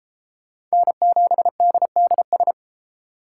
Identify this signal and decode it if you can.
Morse code, heard as a keyed tone, about 30 wpm, 700 Hz. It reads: N7DDS